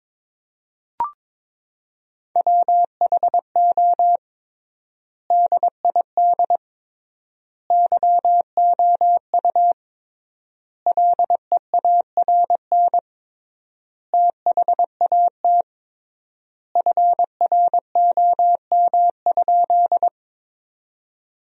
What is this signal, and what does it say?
Morse code, heard as a keyed tone, about 22 wpm, 700 Hz. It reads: WHO DID YOU LEARN THAT FROM?
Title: 3x Word-Spacing - Sentences from Top 300 Words - 22wpm